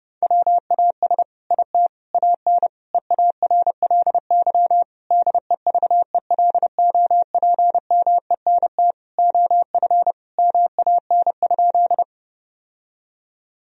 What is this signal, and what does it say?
Morse code, heard as a keyed tone, about 30 wpm, 700 Hz. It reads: WAS IT AN EARLY DEVELOPMENT OF MAN?
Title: Sentences from Top 1000 Words - Review Part 4 - 30wpm